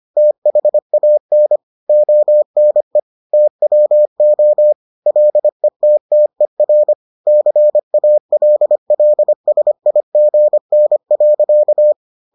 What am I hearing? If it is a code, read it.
Morse code, heard as a keyed tone, about 25 wpm, 600 Hz. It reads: THAN ONE TWO LETTER CALLSIGN.